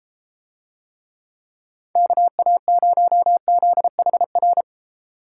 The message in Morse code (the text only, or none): KA0ZHR